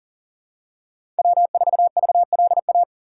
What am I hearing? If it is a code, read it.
Morse code, heard as a keyed tone, about 40 wpm, 700 Hz. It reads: W4VLA
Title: USA Call Signs - 5 Characters - 3x Character Spacing - 40wpm